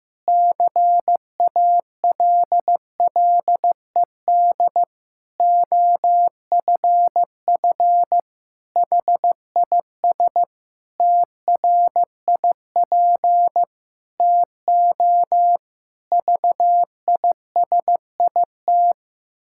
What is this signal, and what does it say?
Morse code, heard as a keyed tone, about 15 wpm, 700 Hz. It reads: CALLED OFF HIS TRIP TO VISIT
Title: Pangram Challenge v2 - 15wpm